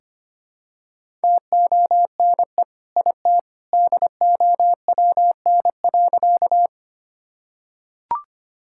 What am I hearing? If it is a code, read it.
Morse code, heard as a keyed tone, about 25 wpm, 700 Hz. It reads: TONE IT DOWN.
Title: Sentences from Top 1600 Words 25wpm